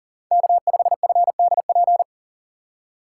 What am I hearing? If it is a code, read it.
Morse code, heard as a keyed tone, about 40 wpm, 700 Hz. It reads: K5FDP
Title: USA Call Signs - 5 Characters - 3x Character Spacing - 40wpm